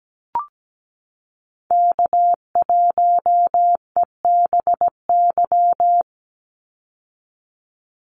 Morse code, heard as a keyed tone, about 17 wpm, 700 Hz. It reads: K1EBY
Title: Call Signs - 17wpm